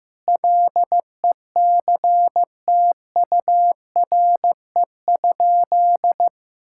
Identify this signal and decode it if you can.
Morse code, heard as a keyed tone, about 16 wpm, 700 Hz. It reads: LECTURE?
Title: Sentences from Top 1600 Words 15wpm